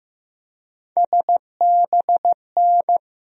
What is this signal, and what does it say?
Morse code, heard as a keyed tone, about 16 wpm, 700 Hz. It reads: SBN
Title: USA Call Sign Postfixes - 3 Letters 15wpm